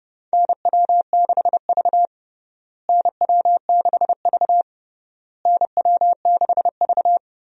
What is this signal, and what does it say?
Morse code, heard as a keyed tone, about 29 wpm, 700 Hz. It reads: NW6V NW6V NW6V
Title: USA Call Signs - 4 Characters - Repeated 3x - 30wpm